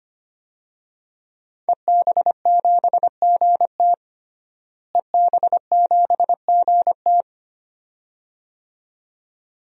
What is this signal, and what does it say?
Morse code, heard as a keyed tone, about 25 wpm, 700 Hz. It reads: EB7GT EB7GT
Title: DXCC-100 Call Signs - 5 Characters - Repeated 2x - 25wpm